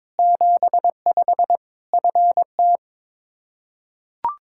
Morse code, heard as a keyed tone, about 22 wpm, 700 Hz. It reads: 75 FT
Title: Practice QSO - Exchange 2 - Info - 22wpm